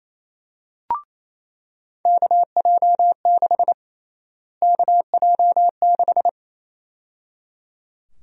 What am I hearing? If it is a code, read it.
Morse code, heard as a keyed tone, about 28 wpm, 700 Hz. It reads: KJ6 KJ6
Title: USA Call Sign Prefixes - Repeated 2x - 28wpm